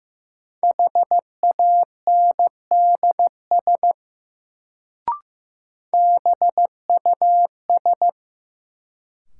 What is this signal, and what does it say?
Morse code, heard as a keyed tone, about 15 wpm, 700 Hz. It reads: HANDS BUS
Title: Sentences from Top 1000 Words - Review Part 6 - New Word Drill - 15wpm